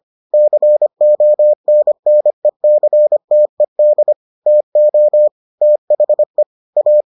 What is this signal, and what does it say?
Morse code, heard as a keyed tone, about 25 wpm, 600 Hz. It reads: CONNECTED TO THE A